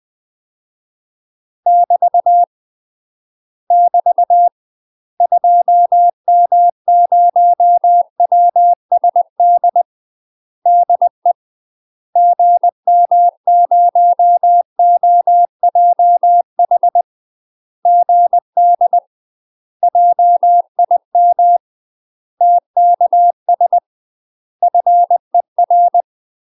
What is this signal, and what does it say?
Morse code, heard as a keyed tone, about 20 wpm, 700 Hz. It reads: <BT> <BT> 2M0WSD DE GM0OJH GD JIM TKS FER
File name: UK MASTER QSOs TRAINING / 20WPM / UK MASTER QSOs 0046